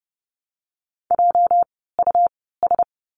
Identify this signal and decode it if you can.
Morse code, heard as a keyed tone, about 30 wpm, spaced as for 17 wpm, 700 Hz. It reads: JUS
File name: course / 30:10wpm / 230 Lesson 37f - Letter x - Words 30wpm